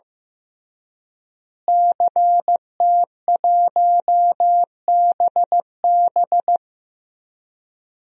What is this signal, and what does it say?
Morse code, heard as a keyed tone, about 16 wpm, 700 Hz. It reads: CT1BB
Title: DXCC-100 Call Signs - 5 Characters - Repeated 2x - 15wpm